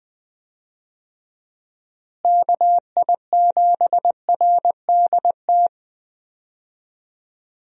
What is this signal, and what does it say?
Morse code, heard as a keyed tone, about 20 wpm, 700 Hz. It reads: KI7RDT
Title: USA Call Signs - 6 Characters - Repeated 2x - 20wpm